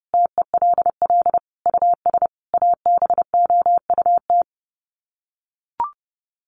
Morse code, heard as a keyed tone, about 30 wpm, 700 Hz. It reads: TELL US ABOUT
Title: Sets of 5 Words Encore2 30wpm